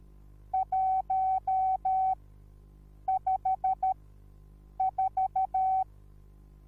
Morse code, heard as a keyed tone, about 13 wpm, spaced as for 7 wpm, 760 Hz. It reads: 154
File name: Onlinelehrgang / morsen / DJ4UF Morsen21